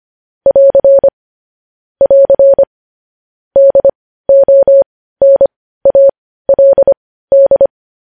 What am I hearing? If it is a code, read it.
Morse code, heard as a keyed tone, about 25 wpm, spaced as for 15 wpm, 550 Hz. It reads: <AR> <AR> DONALD
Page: news-headlines-in-morse-code-at-15-wpm